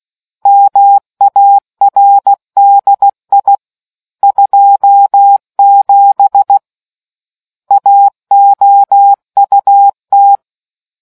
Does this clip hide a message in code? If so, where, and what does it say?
Yes: Morse code from 0.5 to 10.4 s, heard as a keyed tone, about 16 wpm, 800 Hz. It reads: MARDI 27 AOUT